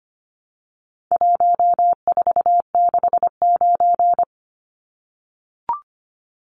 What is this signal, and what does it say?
Morse code, heard as a keyed tone, about 25 wpm, 700 Hz. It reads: 1469
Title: Numbers - 4-Digit 25wpm